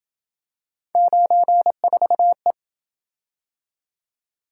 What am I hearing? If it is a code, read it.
Morse code, heard as a keyed tone, about 26 wpm, 700 Hz. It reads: 94E